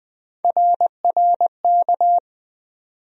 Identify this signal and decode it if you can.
Morse code, heard as a keyed tone, about 20 wpm, 700 Hz. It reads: RRK
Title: USA Call Sign Postfixes - 3 Letters 20wpm